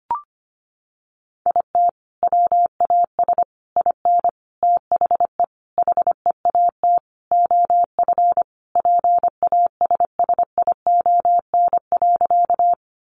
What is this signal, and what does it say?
Morse code, heard as a keyed tone, about 25 wpm, 700 Hz. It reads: IT WAS IN THE HEAT OF PASSION.
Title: Sentences from Top 1300 Words 25wpm